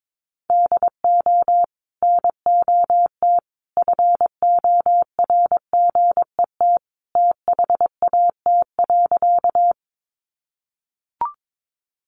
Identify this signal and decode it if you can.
Morse code, heard as a keyed tone, about 22 wpm, 700 Hz. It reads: DO NOT FORGET THAT.
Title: Sentences from Top 1200 Words 22wpm